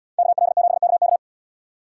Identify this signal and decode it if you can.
Morse code, heard as a keyed tone, about 78 wpm, 700 Hz. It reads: KL7KK